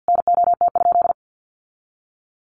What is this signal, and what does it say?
Morse code, heard as a keyed tone, about 50 wpm, 700 Hz. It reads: NOT?
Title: Sentences from Top 1100 Words 50wpm